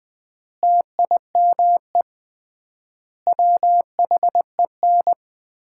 Morse code, heard as a keyed tone, about 21 wpm, 700 Hz. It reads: TIME WHEN
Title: 3x Word-Spacing - Sets of 5 Words - 20wpm